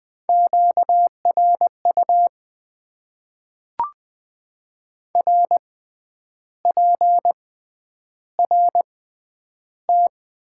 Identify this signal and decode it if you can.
Morse code, heard as a keyed tone, about 20 wpm, spaced as for 9 wpm, 700 Hz. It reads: QRU RPRT
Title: Top 90 QSO Elements - 6x Character Spacing - 20wpm